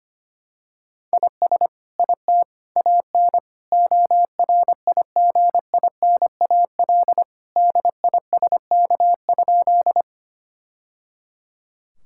Morse code, heard as a keyed tone, about 25 wpm, 700 Hz. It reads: IS IT AN ORIGINAL DISK?
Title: Sentences from Top 1000 Words - Review Part 6 - 25wpm